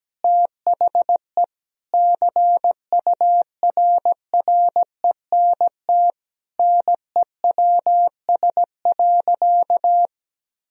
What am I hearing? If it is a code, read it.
Morse code, heard as a keyed tone, about 17 wpm, 700 Hz. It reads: THE CURRENT NEWS.
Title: Sentences from Top 1000 Words - Review Part 4 - 17wpm